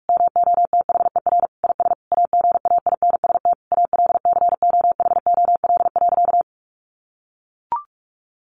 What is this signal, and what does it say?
Morse code, heard as a keyed tone, about 44 wpm, 700 Hz. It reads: MOTHER IS AGAINST ALCOHOL.
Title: Sentences from Top 1400 Words 45wpm